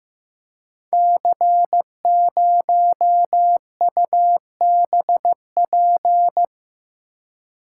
Morse code, heard as a keyed tone, about 15 wpm, 700 Hz. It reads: C0UBP
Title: Code Groups - 5 Letter:Numbers - 15wpm